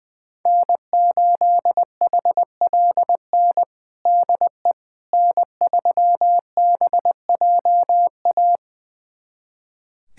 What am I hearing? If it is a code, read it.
Morse code, heard as a keyed tone, about 20 wpm, 700 Hz. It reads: N8HLN DE N3BJA
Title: USA Callsign DE Callsign - 5 Characters - 20wpm